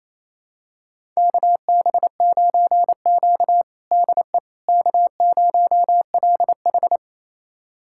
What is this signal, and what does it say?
Morse code, heard as a keyed tone, about 28 wpm, 700 Hz. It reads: KB9Q DE K0LH
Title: USA Callsign DE Callsign - 4 Characters - 28wpm